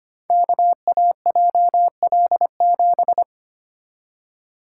Morse code, heard as a keyed tone, about 25 wpm, 700 Hz. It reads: KAJL7
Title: Code Groups - 5 Letter:Numbers - 25wpm